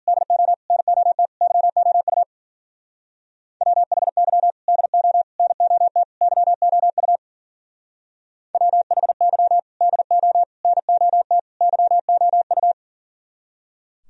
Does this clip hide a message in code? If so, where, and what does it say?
Yes: Morse code from 0.1 to 12.7 s, heard as a keyed tone, about 51 wpm, 700 Hz. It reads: DO NOT YOU WHY DO NOT YOU WHY DO NOT YOU